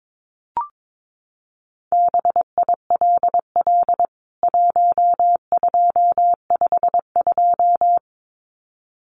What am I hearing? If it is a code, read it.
Morse code, heard as a keyed tone, about 22 wpm, 700 Hz. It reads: BILL 1252
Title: CWT Exchange 22wpm